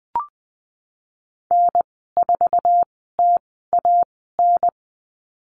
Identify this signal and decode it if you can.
Morse code, heard as a keyed tone, about 19 wpm, spaced as for 14 wpm, 700 Hz. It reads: N4TAN
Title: 055 Lesson 10g - Letter r - Call Signs 20wpm